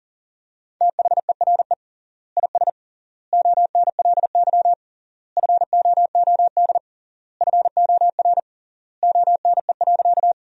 Morse code, heard as a keyed tone, about 40 wpm, 700 Hz. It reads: THERE IS ONLY FOOD FOR ONE.
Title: 3x Word-Spacing - Sentences from Top 300 Words - 40wpm